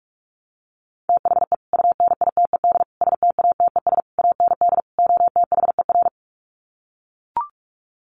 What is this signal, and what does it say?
Morse code, heard as a keyed tone, about 44 wpm, 700 Hz. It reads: THE UNITED STATES AND OTHER
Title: Sets of 5 Words Encore 45wpm